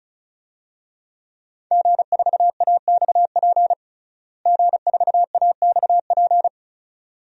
Morse code, heard as a keyed tone, about 36 wpm, 700 Hz. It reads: G4AXP G4AXP
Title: DXCC-100 Call Signs - 5 Characters - Repeated 2x - 35wpm